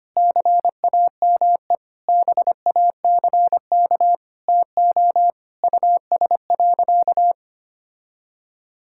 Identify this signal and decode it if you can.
Morse code, heard as a keyed tone, about 25 wpm, 700 Hz. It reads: CAME BACK TO US.
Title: Sentences from Top 1000 Words - Review Part 6 - 25wpm